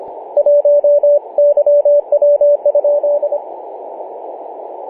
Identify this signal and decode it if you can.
Morse code, heard as a keyed tone, about 26 wpm, 580 Hz, with radio noise. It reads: 1 YW?
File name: posts / oct-08-2033z-tx5z-21-008mhz-cw